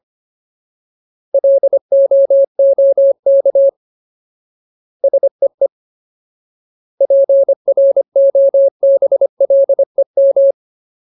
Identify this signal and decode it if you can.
Morse code, heard as a keyed tone, about 25 wpm, 550 Hz. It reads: LOOK SEE PROBLEM
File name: Ultimate Word List / 25WPM Ultimate